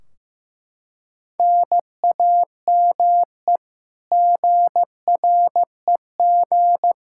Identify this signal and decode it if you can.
Morse code, heard as a keyed tone, about 15 wpm, 700 Hz. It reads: NAME GREG
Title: Practice QSO - Exchange 1 - Beginning - 15wpm